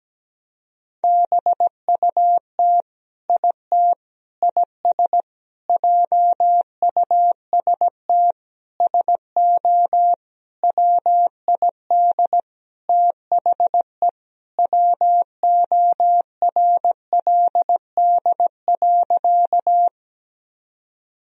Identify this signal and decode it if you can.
Morse code, heard as a keyed tone, about 17 wpm, 700 Hz. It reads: BUT IT IS JUST SO WID THE WORLD.
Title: Sentences from Top 200 Words and Abbreviations 17wpm